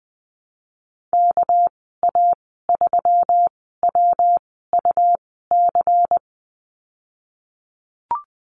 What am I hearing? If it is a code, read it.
Morse code, heard as a keyed tone, about 20 wpm, spaced as for 15 wpm, 700 Hz. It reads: KA3WUC
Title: 231 Lesson 37g - Letter x - Call Signs 20wpm